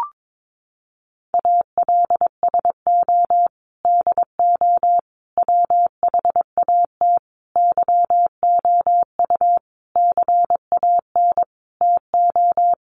Morse code, heard as a keyed tone, about 22 wpm, 700 Hz. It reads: ALSO DO WHAT YOU CAN TO
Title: Sentences from Top 1300 Words 22wpm